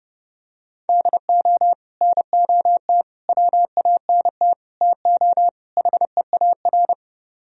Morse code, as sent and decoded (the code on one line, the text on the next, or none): -.. --- / -. --- - / .-- .- -. - / - --- / .... .- .-.
DO NOT WANT TO HEAR